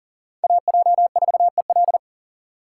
Morse code, heard as a keyed tone, about 37 wpm, 700 Hz. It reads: AJ4EL
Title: USA Call Signs - 5 Characters - 2x Character Spacing - 40wpm